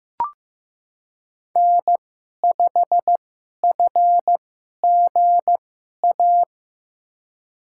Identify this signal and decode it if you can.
Morse code, heard as a keyed tone, about 15 wpm, spaced as for 11 wpm, 700 Hz. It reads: N5FGA
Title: USA Call Signs - 2x Character Spacing - 15wpm